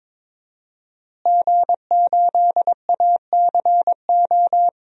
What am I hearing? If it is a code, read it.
Morse code, heard as a keyed tone, about 21 wpm, 700 Hz. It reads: G8ACO
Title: DXCC-100 Call Signs - 5 Characters - Repeated 3x - 22wpm